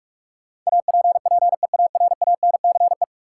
Morse code, heard as a keyed tone, about 46 wpm, 700 Hz. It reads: APPEARANCE
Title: Sentences from Top 1100 Words - New Word Drill Alt - 45wpm